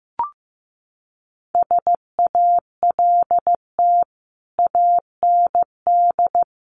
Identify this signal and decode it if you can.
Morse code, heard as a keyed tone, about 15 wpm, 700 Hz. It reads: SALT AND
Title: Sets of 5 Words Encore 15wpm